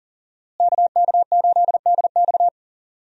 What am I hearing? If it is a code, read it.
Morse code, heard as a keyed tone, about 39 wpm, 700 Hz. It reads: KK8DX
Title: USA Call Signs - 5 Characters - Repeated 2x - 40wpm